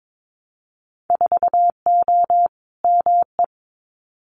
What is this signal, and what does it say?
Morse code, heard as a keyed tone, about 22 wpm, 700 Hz. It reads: 4O ME
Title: Winter Field Day Exchanges Encore 22wpm